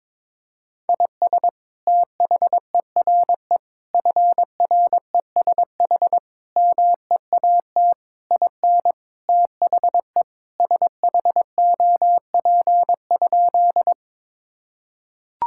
Morse code, heard as a keyed tone, about 22 wpm, 700 Hz. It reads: IS THERE FRESH MEAT IN THE SHOP?
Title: Sentences from Top 1600 Words 22wpm